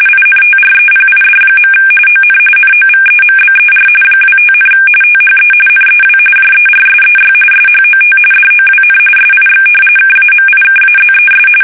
RTTY, about 45.45 baud. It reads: INTREPID. ON NOVEMBER 25, THE NEW JERSEY SHOT DOWN SIX KAMIKAZES
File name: fsk850 / battleshipnj